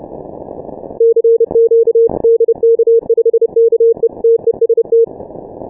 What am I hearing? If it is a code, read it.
Morse code, heard as a keyed tone, about 30 wpm, 460 Hz. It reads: CQ DK5KE TEST
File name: dk5ke / Contest2QSOs 150BpM